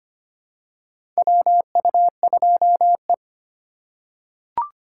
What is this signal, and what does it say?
Morse code, heard as a keyed tone, about 25 wpm, 700 Hz. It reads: WU2E
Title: USA Call Signs - 4 Characters - 2x Character Spacing - 25wpm